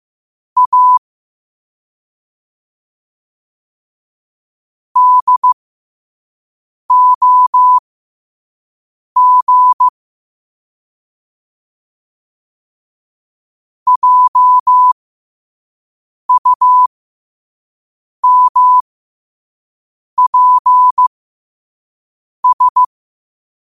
Morse code, heard as a keyed tone, about 15 wpm, spaced as for 5 wpm, 1000 Hz. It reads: A DOG JUMPS
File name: qotdfiles / 5wpm